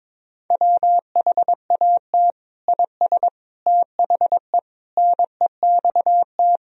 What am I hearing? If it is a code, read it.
Morse code, heard as a keyed tone, about 22 wpm, 700 Hz. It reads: WHAT IS THE NEXT
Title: Sentences from Top 1000 Words - Review Part 2 - 22wpm